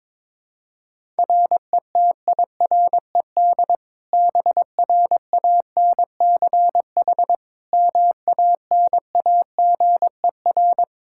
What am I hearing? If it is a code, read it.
Morse code, heard as a keyed tone, about 22 wpm, 700 Hz. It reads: RETIRED BRANCH MANAGER
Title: Practice QSO - Exchange 3 - More Info - 22wpm